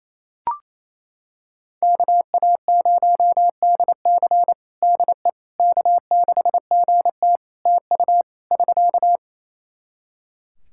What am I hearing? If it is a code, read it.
Morse code, heard as a keyed tone, about 28 wpm, 700 Hz. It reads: KA0DC DE K6GT TU <SK>
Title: Practice QSO - Exchange 4 - Ending - 28wpm